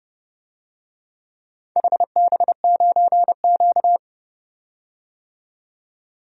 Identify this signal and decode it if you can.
Morse code, heard as a keyed tone, about 30 wpm, 700 Hz. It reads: HB9Q
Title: DXCC-100 Call Signs - 4 Characters - 30wpm